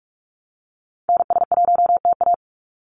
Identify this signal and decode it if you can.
Morse code, heard as a keyed tone, about 46 wpm, 700 Hz. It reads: NS1TA
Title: NA SOTA Activator Callsigns 45wpm